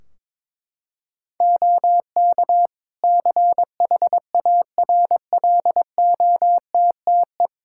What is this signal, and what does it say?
Morse code, heard as a keyed tone, about 22 wpm, 700 Hz. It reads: OK CHARLOTTE
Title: Practice QSO - Exchange 3 - More Info - 22wpm